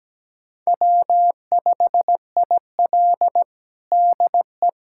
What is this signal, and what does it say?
Morse code, heard as a keyed tone, about 17 wpm, 700 Hz. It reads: W5IL DE
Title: USA Callsign DE Callsign - 4 Characters - 17wpm